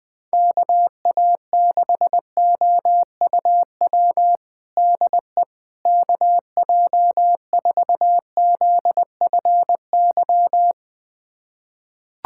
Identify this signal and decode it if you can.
Morse code, heard as a keyed tone, about 21 wpm, 700 Hz. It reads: KA6OUW DE KJ4ZFY
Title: USA Callsign DE Callsign - 6 Characters - 20wpm